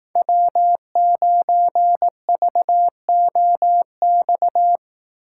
Morse code, heard as a keyed tone, about 18 wpm, 700 Hz. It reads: W9VOX
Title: Speed Racing - USA 1x3 Call Signs 15wpm